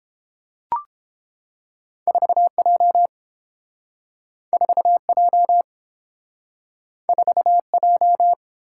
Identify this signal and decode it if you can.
Morse code, heard as a keyed tone, about 30 wpm, 700 Hz. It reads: 4J 4J 4J